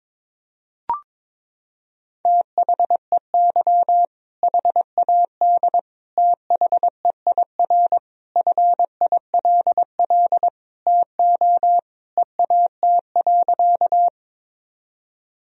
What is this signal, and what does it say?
Morse code, heard as a keyed tone, about 22 wpm, 700 Hz. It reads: THEY HAD THEIR FILL TO EAT.
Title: Sentences from Top 1000 Words - Review Part 3 - 22wpm